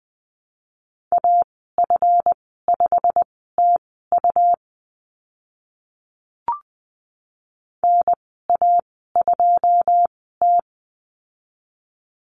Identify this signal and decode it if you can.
Morse code, heard as a keyed tone, about 20 wpm, spaced as for 15 wpm, 700 Hz. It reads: AF5TU NA2T
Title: 154 Lesson 25g - Letter p - Call Signs 20wpm